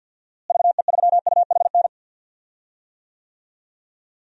Morse code, heard as a keyed tone, about 45 wpm, 700 Hz. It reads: VE3UHN
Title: DXCC-100 Call Signs - 6 Characters - Repeated 2x - 50wpm